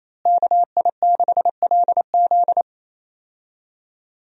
Morse code, heard as a keyed tone, about 28 wpm, 700 Hz. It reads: KI6LZ
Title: USA Call Signs - 5 Characters - Repeated 3x - 28wpm